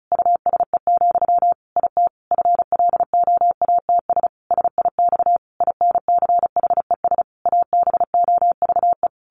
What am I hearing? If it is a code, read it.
Morse code, heard as a keyed tone, about 34 wpm, 700 Hz. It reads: USE, IT FLOATS SIX INCHES ABOVE
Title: A Sound Of Thunder 35wpm